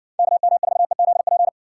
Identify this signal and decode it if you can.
Morse code, heard as a keyed tone, about 60 wpm, 700 Hz. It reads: KM4EZP